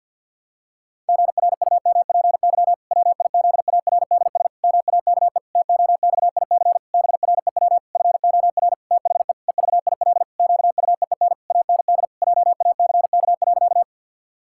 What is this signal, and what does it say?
Morse code, heard as a keyed tone, about 51 wpm, 700 Hz. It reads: GRUMPY WIZARDS MAKE TOXIC BREW FOR THE EVIL QUEEN AND JACK.
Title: Pangram Challenge v2 - 50wpm